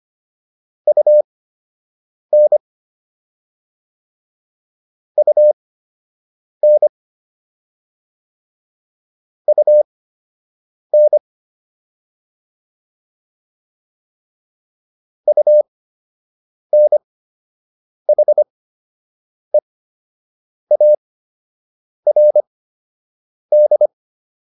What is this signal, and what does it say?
Morse code, heard as a keyed tone, about 25 wpm, spaced as for 7 wpm, 600 Hz. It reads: UN UN UN UNHEARD